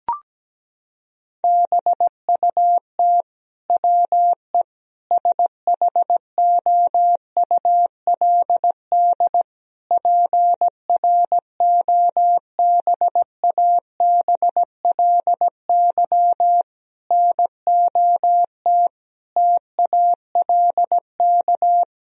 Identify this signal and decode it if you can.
Morse code, heard as a keyed tone, about 17 wpm, 700 Hz. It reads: BUT WE SHOULD PROBABLY NOT TALK
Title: Sentences from Top 1000 Words - Review Part 4 - 17wpm